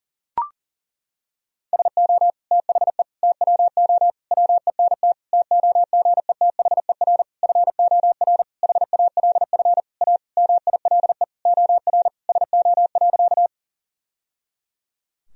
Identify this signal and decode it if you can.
Morse code, heard as a keyed tone, about 40 wpm, 700 Hz. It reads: SO THE TWO WENT TOGETHER FOR HALF A MILE OR SO.
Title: Sentences from Top 1000 Words - Review Part 1 - 40wpm